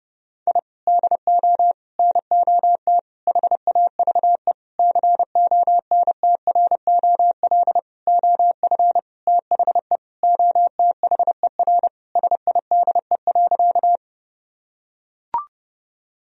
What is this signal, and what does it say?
Morse code, heard as a keyed tone, about 30 wpm, 700 Hz. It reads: I DO NOT HAVE CONTROL OF THE OTHER SIDE.
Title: Sentences from Top 1000 Words - Review Part 3 - 30wpm